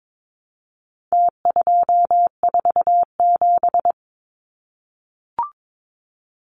Morse code, heard as a keyed tone, about 22 wpm, 700 Hz. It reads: T247
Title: Numbers - Contest Style - 4-Digit 22wpm